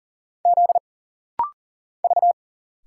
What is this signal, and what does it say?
Morse code, heard as a keyed tone, about 40 wpm, 700 Hz. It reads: Z V
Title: Single Letters - Rapid-Fire 40wpm